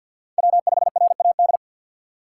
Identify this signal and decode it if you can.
Morse code, heard as a keyed tone, about 62 wpm, 700 Hz. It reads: W5RAD
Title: Speed Racing - USA 1x3 Call Signs 50wpm